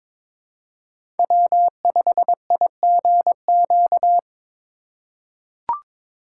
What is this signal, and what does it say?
Morse code, heard as a keyed tone, about 22 wpm, 700 Hz. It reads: W5IGQ
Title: USA Call Signs - 5 Characters - 4x Character Spacing - 22wpm